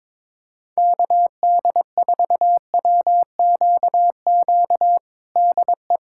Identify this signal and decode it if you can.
Morse code, heard as a keyed tone, about 22 wpm, 700 Hz. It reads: KD4WQQ DE
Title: USA Callsign DE Callsign - 6 Characters - 22wpm